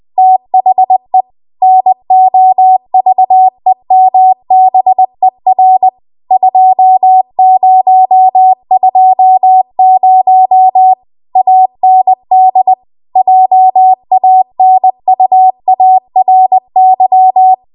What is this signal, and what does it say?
Morse code, heard as a keyed tone, about 20 wpm, 750 Hz. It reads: THE NOVEMBER 2020 AND JANUARY